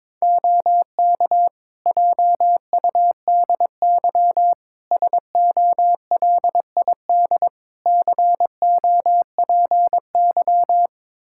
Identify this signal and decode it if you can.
Morse code, heard as a keyed tone, about 22 wpm, 700 Hz. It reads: OK JUDY SOLID COPY
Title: Practice QSO - Exchange 3 - More Info - 22wpm